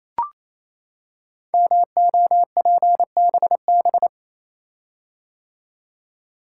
Morse code, heard as a keyed tone, about 27 wpm, 700 Hz. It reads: MOPBB
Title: Code Groups - 5 Letters - 28wpm